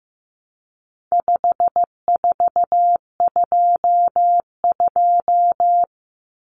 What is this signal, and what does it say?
Morse code, heard as a keyed tone, about 15 wpm, 700 Hz. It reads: 5422
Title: Numbers - 4-Digit 15wpm